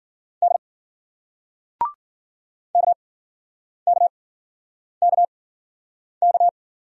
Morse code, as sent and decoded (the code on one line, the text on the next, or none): -.. / -.- -.- -.- -.-
D KKKK